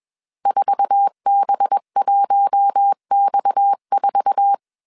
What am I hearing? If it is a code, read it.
Morse code, heard as a keyed tone, about 21 wpm, 800 Hz. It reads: 4B1X4
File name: CW / DET / 1 9 1500C S C 20wpm 2310